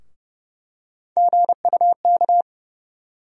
Morse code, heard as a keyed tone, about 32 wpm, 700 Hz. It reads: GUK
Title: USA Call Sign Postfixes - 3 Letters 30wpm